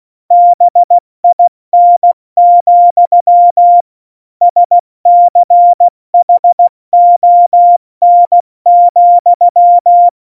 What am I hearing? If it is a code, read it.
Morse code, heard as a keyed tone, about 16 wpm, 700 Hz. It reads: BIN, SCHON,
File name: Morsekurs / Witze / Witz 14. Januar 2020, 70 F80